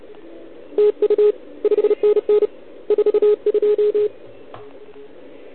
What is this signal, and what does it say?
Morse code, heard as a keyed tone, about 27 wpm, 410 Hz. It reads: TU 5NN 42
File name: ARCHIV / OKDX2020 / OL4Wa